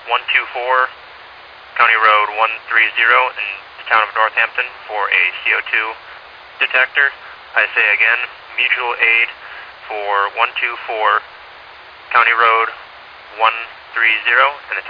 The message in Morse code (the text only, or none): none